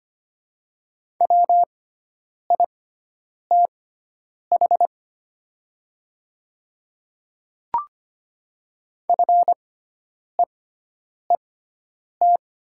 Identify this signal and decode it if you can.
Morse code, heard as a keyed tone, about 25 wpm, spaced as for 9 wpm, 700 Hz. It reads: WITH FEET
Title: 6x Character Spacing - Top 300 Words - 25wpm